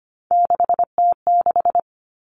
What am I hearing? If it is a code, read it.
Morse code, heard as a keyed tone, about 25 wpm, 700 Hz. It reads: 6T6
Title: All Cut Numbers - 3-Digit 25wpm